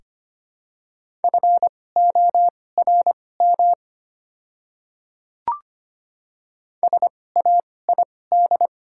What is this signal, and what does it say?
Morse code, heard as a keyed tone, about 25 wpm, spaced as for 18 wpm, 700 Hz. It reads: FORM SAID
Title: 2x Character Spacing - Top 200 Words - 25wpm